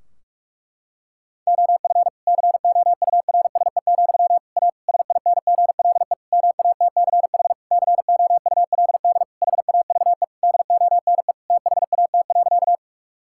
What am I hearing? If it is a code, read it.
Morse code, heard as a keyed tone, about 47 wpm, 700 Hz. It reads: OF COURSE, A SINGLE MATCH COULD HAVE DONE THAT.
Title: Sentences from Top 1600 Words 45wpm